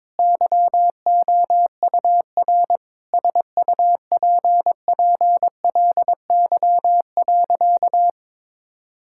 Morse code, heard as a keyed tone, about 22 wpm, 700 Hz. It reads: YOUR SUPPLY.
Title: Sentences from Top 1000 Words - Review Part 5 - 22wpm